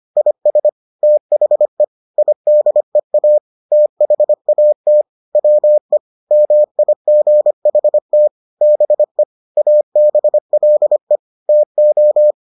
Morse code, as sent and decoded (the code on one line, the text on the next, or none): .. ... / - .... / .. -.. .- / - .... .- - / .-- . / -- .. --. .... - / -... / .- -... .-.. / - ---
IS THE IDEA THAT WE MIGHT BE ABLE TO